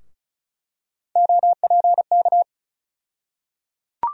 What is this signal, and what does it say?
Morse code, heard as a keyed tone, about 34 wpm, 700 Hz. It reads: OPK
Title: USA Call Sign Postfixes - 3 Letters 35wpm